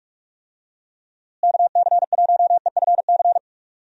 Morse code, heard as a keyed tone, about 46 wpm, 700 Hz. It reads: KC1EFC
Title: USA Call Signs - 6 Characters - Repeated 3x - 45wpm